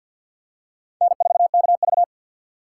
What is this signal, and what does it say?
Morse code, heard as a keyed tone, about 51 wpm, 700 Hz. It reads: N4KV
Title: USA Call Signs - 3x Character Spacing - 50wpm